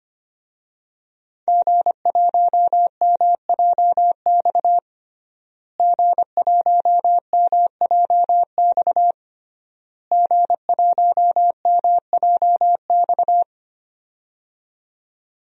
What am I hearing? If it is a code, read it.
Morse code, heard as a keyed tone, about 25 wpm, 700 Hz. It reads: G1MJX G1MJX G1MJX
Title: DXCC-100 Call Signs - 5 Characters - Repeated 3x - 25wpm